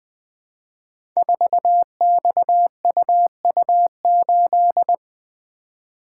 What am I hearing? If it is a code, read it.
Morse code, heard as a keyed tone, about 20 wpm, 700 Hz. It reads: 4XUU8
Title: Code Groups - 5 Letter:Numbers - 20wpm